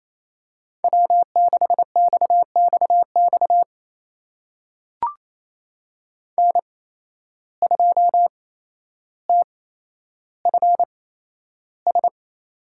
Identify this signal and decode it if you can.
Morse code, heard as a keyed tone, about 27 wpm, spaced as for 12 wpm, 700 Hz. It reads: W6XXX N2TFS
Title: USA Call Signs - 5 Characters - 8x Character Spacing - 28wpm